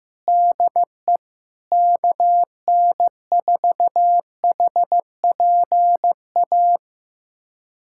Morse code, heard as a keyed tone, about 15 wpm, 700 Hz. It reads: DE KN4HPA
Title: USA Callsign DE Callsign - 6 Characters - 15wpm